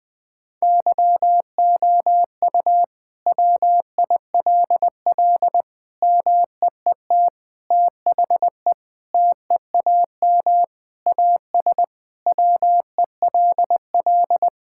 Morse code, heard as a keyed tone, about 20 wpm, 700 Hz. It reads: YOU WILL MEET THE TEAM AS WELL
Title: Sentences with 6-Word Phrases - 20wpm